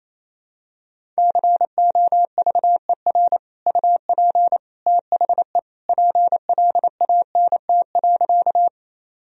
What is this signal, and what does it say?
Morse code, heard as a keyed tone, about 28 wpm, 700 Hz. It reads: COVER UP THE PLANT.
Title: Sentences from Top 1000 Words - Review Part 1 - 28wpm